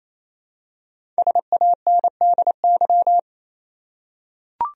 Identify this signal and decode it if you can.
Morse code, heard as a keyed tone, about 28 wpm, 700 Hz. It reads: SANDY
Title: USA Common Names - Max 5 Letters - 28wpm